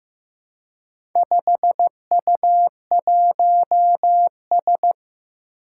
Morse code, heard as a keyed tone, about 15 wpm, 700 Hz. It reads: 5U1S
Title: Four Challenging Letter-Number Combinations 15wpm